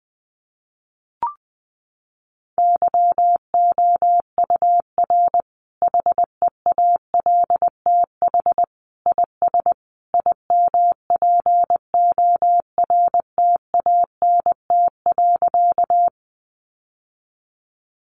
Morse code, heard as a keyed tone, about 20 wpm, 700 Hz. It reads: YOUR HEALTH IS IMPORTANT.
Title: Sentences from Top 600 Words 20wpm